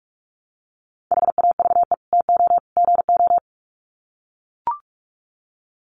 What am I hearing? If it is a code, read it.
Morse code, heard as a keyed tone, about 44 wpm, 700 Hz. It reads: HAVE TO GO
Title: Sets of 3 Words 45wpm